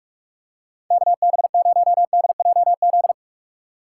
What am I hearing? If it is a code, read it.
Morse code, heard as a keyed tone, about 46 wpm, 700 Hz. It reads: KB0DJZ
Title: USA Call Signs - 6 Characters - 2x Character Spacing - 45wpm